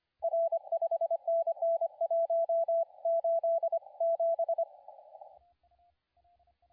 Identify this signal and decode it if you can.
Morse code, heard as a keyed tone, about 25 wpm, 670 Hz, with radio noise. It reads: R 5NN 187
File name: la4c / sac-cw-2017 / CONTEST AUDIO 2100Z